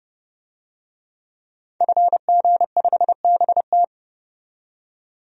Morse code, heard as a keyed tone, about 30 wpm, 700 Hz. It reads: FG5BT
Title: DXCC-100 Call Signs - 5 Characters - 30wpm